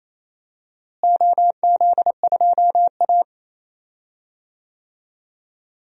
Morse code, heard as a keyed tone, about 28 wpm, 700 Hz. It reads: OZ2A
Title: DXCC-100 Call Signs - 4 Characters - Repeated 3x - 28wpm